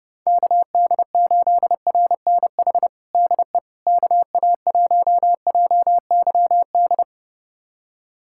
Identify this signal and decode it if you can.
Morse code, heard as a keyed tone, about 30 wpm, 700 Hz. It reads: KD8RNH DE KA1JYD
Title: USA Callsign DE Callsign 30wpm